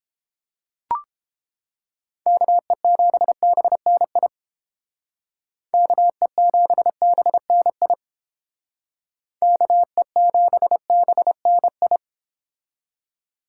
Morse code, heard as a keyed tone, about 29 wpm, 700 Hz. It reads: KE7BNI KE7BNI KE7BNI